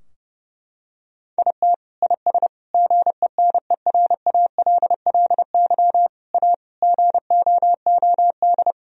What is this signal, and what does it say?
Morse code, heard as a keyed tone, about 30 wpm, 700 Hz. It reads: IT IS GENERALLY A GOOD
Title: Sentences from Top 1000 Words - Review Part 4 - 30wpm